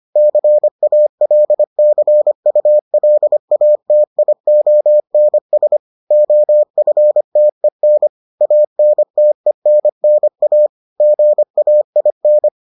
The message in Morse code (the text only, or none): CALCULATIONS OFTEN ANTENNA GAIN